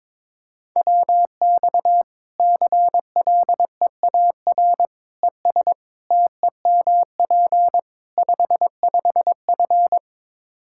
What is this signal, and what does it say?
Morse code, heard as a keyed tone, about 22 wpm, 700 Hz. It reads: WX CLEAR ES TEMP 55F
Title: Practice QSO - Exchange 2 - Info - 22wpm